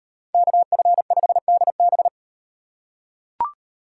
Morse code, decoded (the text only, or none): KF5DB